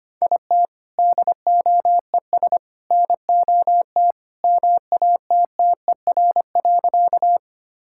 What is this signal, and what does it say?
Morse code, heard as a keyed tone, about 25 wpm, 700 Hz. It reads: IT DOES NOT MATTER.
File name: practice / 25wpm / Sentences with 3-Word Phrases - 25wpm